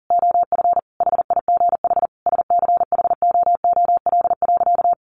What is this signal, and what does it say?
Morse code, heard as a keyed tone, about 40 wpm, 700 Hz. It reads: OF HIGH SCHOOL.
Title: Additional Easy Phrases 40wpm